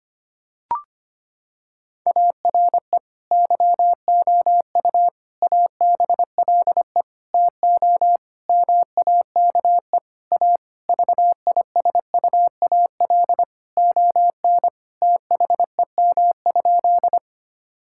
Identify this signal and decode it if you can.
Morse code, heard as a keyed tone, about 25 wpm, 700 Hz. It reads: ARE YOU ABLE TO MAKE A VISUAL ON THEM?
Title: Sentences from Top 1000 Words - Review Part 6 - 25wpm